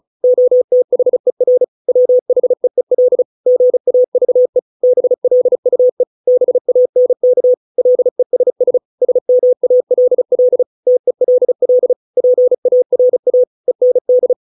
Morse code, heard as a keyed tone, about 34 wpm, 500 Hz. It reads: OTHER WHEEL GAVE BLUE BANK LESS SMALL TELL PARA END
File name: 632 words / 632 Words 35WPM